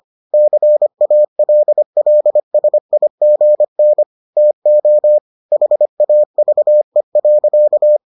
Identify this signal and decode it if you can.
Morse code, heard as a keyed tone, about 25 wpm, 600 Hz. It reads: CALLSIGN TO HAVE.